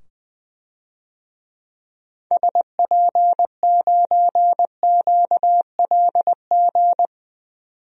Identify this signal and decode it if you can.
Morse code, heard as a keyed tone, about 20 wpm, 700 Hz. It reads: SP9QLG
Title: DXCC-100 Call Signs - 6 Characters - 20wpm